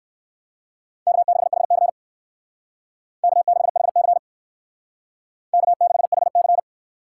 Morse code, heard as a keyed tone, about 59 wpm, 700 Hz. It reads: K6HC K6HC K6HC